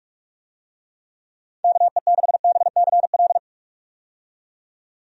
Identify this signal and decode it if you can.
Morse code, heard as a keyed tone, about 46 wpm, 700 Hz. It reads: KE6BCL
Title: USA Call Signs - 6 Characters - 45wpm